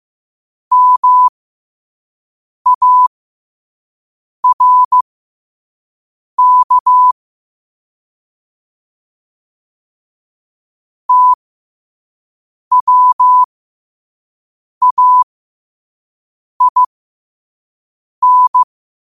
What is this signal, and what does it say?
Morse code, heard as a keyed tone, about 15 wpm, spaced as for 5 wpm, 1000 Hz. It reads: MARK TWAIN
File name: qotdfiles / 5wpm